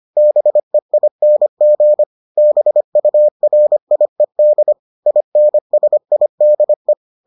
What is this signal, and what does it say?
Morse code, heard as a keyed tone, about 25 wpm, 600 Hz. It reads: BEING BURIED INSIDE